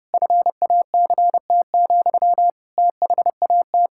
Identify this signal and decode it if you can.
Morse code, heard as a keyed tone, about 30 wpm, 700 Hz. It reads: FACT, THAT
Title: Sentences from Top 1600 Words 30wpm